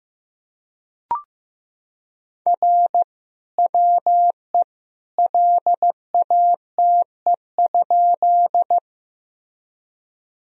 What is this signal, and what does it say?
Morse code, heard as a keyed tone, about 15 wpm, 700 Hz. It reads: R WE LATE?
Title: Sentences from Top 300 Words and Abbreviations 15wpm